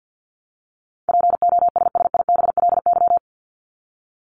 Morse code, heard as a keyed tone, about 50 wpm, 700 Hz. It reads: POSSIBLY
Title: LY Suffix - Top 100 Words with Base Words 50wpm